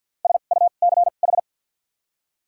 Morse code, heard as a keyed tone, about 53 wpm, spaced as for 37 wpm, 700 Hz. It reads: SUCH
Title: 2x Character Spacing - Top 200 Words - 50wpm